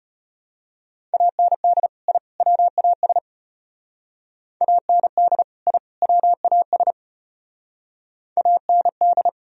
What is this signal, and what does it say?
Morse code, heard as a keyed tone, about 32 wpm, 700 Hz. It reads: AND I WAS AND I WAS AND